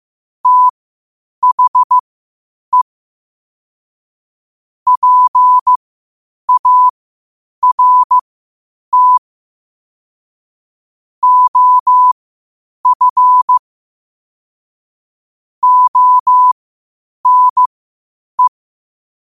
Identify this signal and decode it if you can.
Morse code, heard as a keyed tone, about 15 wpm, spaced as for 8 wpm, 1000 Hz. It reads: THE PART OF ONE
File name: qotdfiles / 8wpm